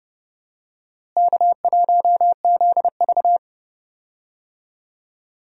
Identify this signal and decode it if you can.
Morse code, heard as a keyed tone, about 31 wpm, 700 Hz. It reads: K1ZV
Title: USA Call Signs - 4 Characters - Repeated 2x - 30wpm